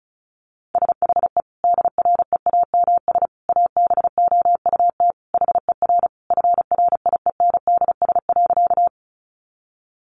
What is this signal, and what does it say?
Morse code, heard as a keyed tone, about 34 wpm, 700 Hz. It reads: SHE DREAMS ABOUT HER FRIENDS.
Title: Easy Phrases 35wpm